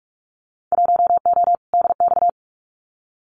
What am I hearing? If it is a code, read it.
Morse code, heard as a keyed tone, about 46 wpm, 700 Hz. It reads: 1O DX
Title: Winter Field Day Exchanges Encore 45wpm